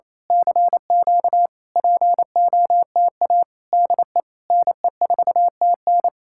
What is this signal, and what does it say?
Morse code, heard as a keyed tone, about 26 wpm, 700 Hz. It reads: CQ POTA DE NE4TN
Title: Speed-Racing Common QSO Phrases Encore - 28wpm